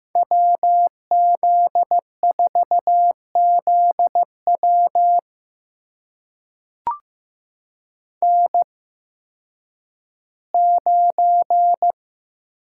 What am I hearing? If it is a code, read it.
Morse code, heard as a keyed tone, about 15 wpm, spaced as for 10 wpm, 700 Hz. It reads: WZ4ZW N9
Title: USA Call Signs - 5 Characters - 8x Character Spacing - 15wpm